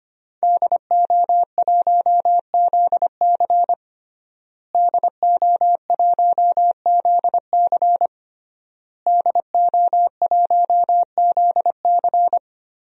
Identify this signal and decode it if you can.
Morse code, heard as a keyed tone, about 25 wpm, 700 Hz. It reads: DO1ZC DO1ZC DO1ZC
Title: DXCC-100 Call Signs - 5 Characters - Repeated 3x - 25wpm